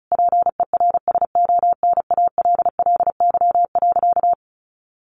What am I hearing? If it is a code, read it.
Morse code, heard as a keyed tone, about 34 wpm, 700 Hz. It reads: PERSONALLY.
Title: Sentences from Top 1400 Words 35wpm